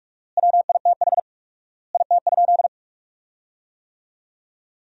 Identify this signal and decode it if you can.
Morse code, heard as a keyed tone, about 46 wpm, 700 Hz. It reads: WITH IT?
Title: 4x Word-Spacing - Sentences from Top 100 Words - 45wpm